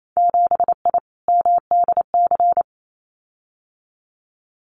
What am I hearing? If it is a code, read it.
Morse code, heard as a keyed tone, about 28 wpm, 700 Hz. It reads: 7I MDC
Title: Winter Field Day Exchanges Encore 28wpm